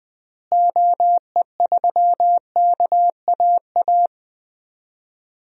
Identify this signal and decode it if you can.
Morse code, heard as a keyed tone, about 20 wpm, 700 Hz. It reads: OE3KAA
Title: DXCC-100 Call Signs - 6 Characters - Repeated 2x - 20wpm